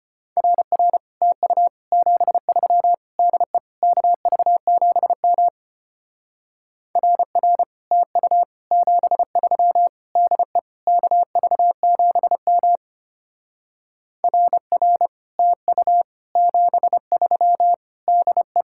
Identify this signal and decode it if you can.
Morse code, heard as a keyed tone, about 32 wpm, 700 Hz. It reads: RR TU 73 DE KV7M RR TU 73 DE KV7M RR TU 73 DE